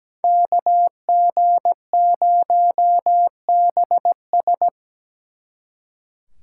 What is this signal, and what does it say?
Morse code, heard as a keyed tone, about 20 wpm, 700 Hz. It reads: KG0BS
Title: Speed Racing - USA 2x2 Call Signs 17wpm